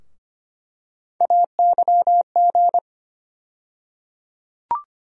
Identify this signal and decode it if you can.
Morse code, heard as a keyed tone, about 25 wpm, 700 Hz. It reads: AYG
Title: USA Call Sign Postfixes - 3 Letters 25wpm